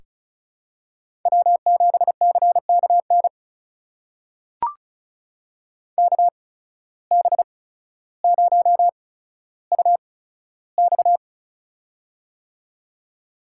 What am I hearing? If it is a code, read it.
Morse code, heard as a keyed tone, about 36 wpm, spaced as for 15 wpm, 700 Hz. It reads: W7CKN KB0UX
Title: USA Call Signs - 5 Characters - 8x Character Spacing - 35wpm